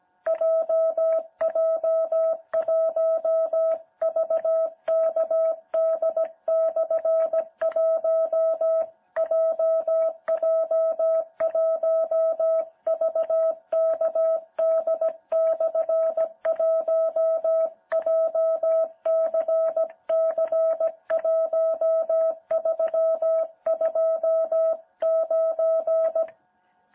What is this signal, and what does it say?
Morse code, heard as a keyed tone, about 17 wpm, 660 Hz. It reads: JJ1VKD/1 JJ1VKD/1 JCC1329
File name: jh8cbh / 2021cw